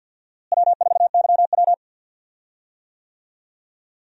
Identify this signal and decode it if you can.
Morse code, heard as a keyed tone, about 52 wpm, 700 Hz. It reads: W4YW
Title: USA Call Signs - 4 Characters - Repeated 2x - 50wpm